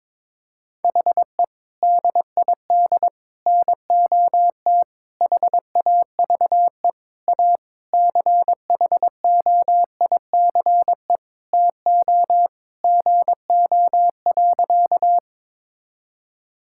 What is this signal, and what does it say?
Morse code, heard as a keyed tone, about 22 wpm, 700 Hz. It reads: HE DID NOT HAVE A CHOICE TO GO.
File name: practice / 22wpm / Sentences with 4-Word Phrases - 22wpm